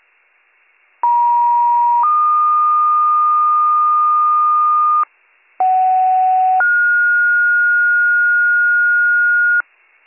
Two-tone page, heard as a one-second tone, then a three-second tone, about 950 then 1220 Hz, about 750 then 1430 Hz, with noise. It is heard from 1.0 to 9.6 s.